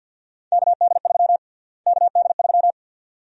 Morse code, heard as a keyed tone, about 53 wpm, 700 Hz. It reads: KD3 KD3
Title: USA Call Sign Prefixes - Repeated 2x - 50wpm